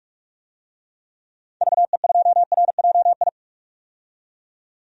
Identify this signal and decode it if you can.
Morse code, heard as a keyed tone, about 43 wpm, 700 Hz. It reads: VE2RJI
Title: DXCC-100 Call Signs - 6 Characters - 45wpm